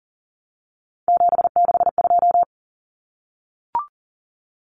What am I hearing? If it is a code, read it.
Morse code, heard as a keyed tone, about 40 wpm, 700 Hz. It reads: 762
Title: Common Cut Numbers - 3-Digit 40wpm